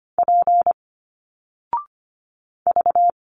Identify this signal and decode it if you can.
Morse code, heard as a keyed tone, about 25 wpm, 700 Hz. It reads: P V
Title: Single Letters 25wpm